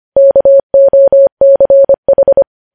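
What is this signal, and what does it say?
Morse code, heard as a keyed tone, about 25 wpm, 550 Hz. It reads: KOCH
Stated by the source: News headlines sent at 25.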